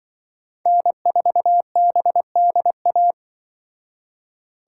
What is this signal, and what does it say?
Morse code, heard as a keyed tone, about 24 wpm, 700 Hz. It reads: N4BDA
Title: Speed Racing - USA 1x3 Call Signs 20wpm